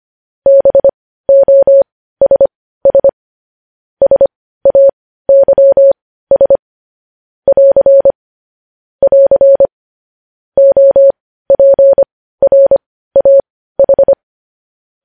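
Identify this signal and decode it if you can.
Morse code, heard as a keyed tone, about 25 wpm, spaced as for 15 wpm, 550 Hz. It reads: BOSS SAYS <AR> <AR> OPRAH
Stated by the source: Morse news headlines sent at 15 words per minute.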